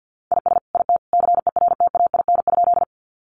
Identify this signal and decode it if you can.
Morse code, heard as a keyed tone, about 50 wpm, 700 Hz. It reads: IS IT CERTAIN?
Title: Sentences from Top 500 Words 50wpm